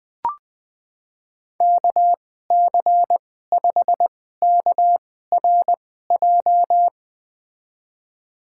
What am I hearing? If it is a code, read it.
Morse code, heard as a keyed tone, about 20 wpm, spaced as for 14 wpm, 700 Hz. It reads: KC5KRJ
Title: USA Call Signs - 6 Characters - 2x Character Spacing - 20wpm